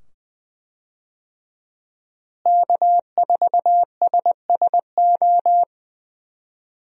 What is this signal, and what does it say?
Morse code, heard as a keyed tone, about 21 wpm, 700 Hz. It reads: K4SSO
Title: USA Call Signs - 5 Characters - Repeated 2x - 20wpm